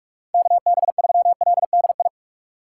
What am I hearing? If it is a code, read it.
Morse code, heard as a keyed tone, about 41 wpm, 700 Hz. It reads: KB3LDI
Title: USA Call Signs - 6 Characters - 3x Character Spacing - 45wpm